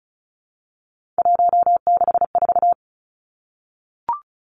Morse code, heard as a keyed tone, about 36 wpm, 700 Hz. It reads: 164
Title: Three Challenging Letter-Number Combinations 35wpm